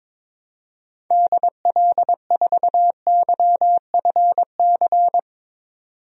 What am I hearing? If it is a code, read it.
Morse code, heard as a keyed tone, about 21 wpm, 700 Hz. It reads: DL4YFC
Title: DXCC-100 Call Signs - 6 Characters - 22wpm